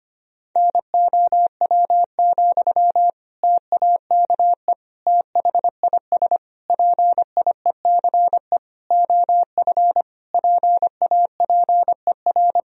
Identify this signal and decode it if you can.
Morse code, heard as a keyed tone, about 25 wpm, 700 Hz. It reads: NOW, TAKE THIS PIECE OF PAPER
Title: Sentences from Top 1000 Words - Review Part 2 - 25wpm